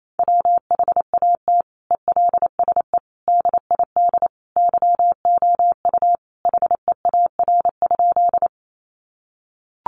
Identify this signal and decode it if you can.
Morse code, heard as a keyed tone, about 28 wpm, 700 Hz. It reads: WHAT ELSE DID YOU HEAR?
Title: Sentences from Top 800 Words 28wpm